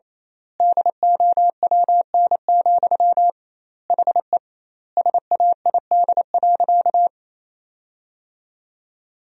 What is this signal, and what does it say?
Morse code, heard as a keyed tone, about 28 wpm, 700 Hz. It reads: DOWN, HE SAID.
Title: 2x Word-Spacing - Sentences from Top 500 Words - 28wpm